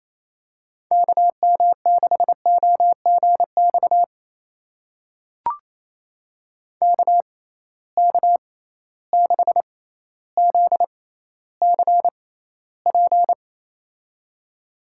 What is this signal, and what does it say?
Morse code, heard as a keyed tone, about 28 wpm, spaced as for 14 wpm, 700 Hz. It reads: KM6OGX KK6ZCP
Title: USA Call Signs - 6 Characters - 6x Character Spacing - 28wpm